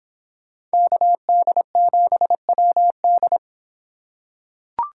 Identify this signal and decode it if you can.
Morse code, heard as a keyed tone, about 26 wpm, 700 Hz. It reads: KD7WD